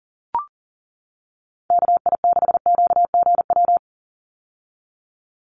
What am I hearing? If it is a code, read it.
Morse code, heard as a keyed tone, about 40 wpm, 700 Hz. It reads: KI6QGW
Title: USA Call Signs 40wpm